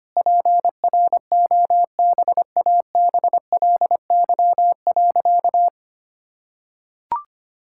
Morse code, heard as a keyed tone, about 25 wpm, 700 Hz. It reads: PROBABLY.
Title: Sentences from Top 1000 Words - Review Part 5 - 25wpm